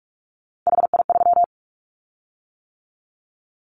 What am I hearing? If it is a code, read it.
Morse code, heard as a keyed tone, about 44 wpm, 700 Hz. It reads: HI3
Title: Callsign Prefixes for Top 100 DXCC - 45wpm